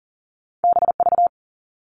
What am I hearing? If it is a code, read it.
Morse code, heard as a keyed tone, about 42 wpm, 700 Hz. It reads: BV
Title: All Cut Numbers - 2-Digit 40wpm